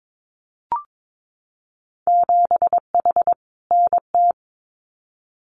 Morse code, heard as a keyed tone, about 22 wpm, 700 Hz. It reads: 7H NT
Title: Winter Field Day Exchanges Encore 22wpm